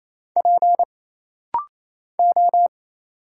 Morse code, heard as a keyed tone, about 27 wpm, 700 Hz. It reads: P O
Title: Single Letters - Rapid-Fire 28wpm